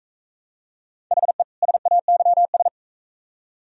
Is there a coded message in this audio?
Yes: Morse code from 1.1 to 2.7 s, heard as a keyed tone, about 42 wpm, 700 Hz. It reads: HE SAYS